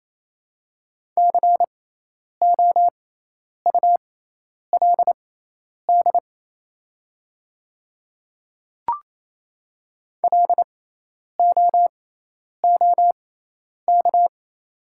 Morse code, heard as a keyed tone, about 28 wpm, spaced as for 10 wpm, 700 Hz. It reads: COULD LOOK
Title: 6x Character Spacing - Top 200 Words - 28wpm